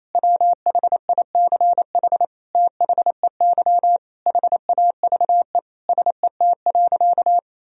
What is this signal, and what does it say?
Morse code, heard as a keyed tone, about 28 wpm, 700 Hz. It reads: WHICH THEY HAVE SET.
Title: Sentences from Top 1000 Words - Review Part 2 - 28wpm